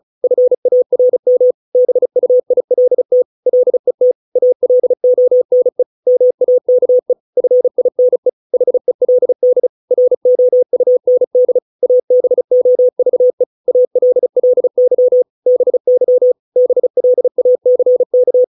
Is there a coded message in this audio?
Yes: Morse code from 0.2 to 18.4 s, heard as a keyed tone, about 34 wpm, 500 Hz. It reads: FARM BUILT LET ALONE MAKE FINE HELD ROUND ABOVE ALLY BY BLACK